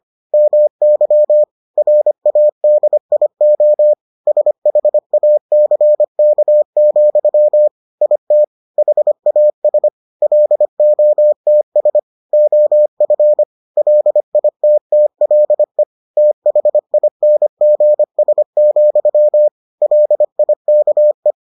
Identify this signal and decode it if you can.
Morse code, heard as a keyed tone, about 25 wpm, 600 Hz. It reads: MY RADIO SHACK, IT HAS LOTS OF LITTLE THINGS, LIKE